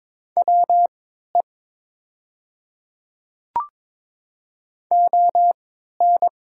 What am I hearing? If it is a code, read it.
Morse code, heard as a keyed tone, about 23 wpm, spaced as for 13 wpm, 700 Hz. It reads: WE ON
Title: 3x Character Spacing - Top 100 Words - 22wpm